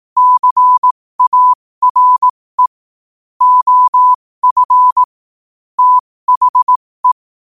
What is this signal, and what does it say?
Morse code, heard as a keyed tone, about 18 wpm, spaced as for 15 wpm, 1000 Hz. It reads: CARE OF THE
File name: qotdfiles / 15wpm